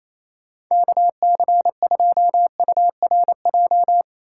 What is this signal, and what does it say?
Morse code, heard as a keyed tone, about 28 wpm, 700 Hz. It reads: KC2URJ
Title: USA Call Signs - 6x Character Spacing - 28wpm